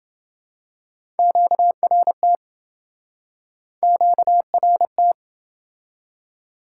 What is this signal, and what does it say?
Morse code, heard as a keyed tone, about 28 wpm, 700 Hz. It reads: QRT QRT